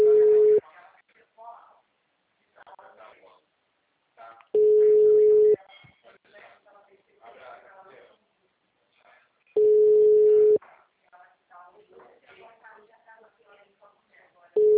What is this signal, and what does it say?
Ringback tone (Europe, 425 Hz), with speech.
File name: out- -2005- - - .97364